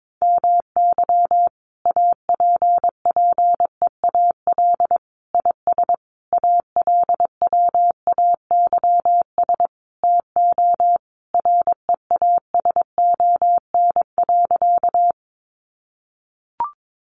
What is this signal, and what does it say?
Morse code, heard as a keyed tone, about 22 wpm, 700 Hz. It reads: MY APPEAL IS ALWAYS TO REASON.
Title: Sentences from Top 1300 Words 22wpm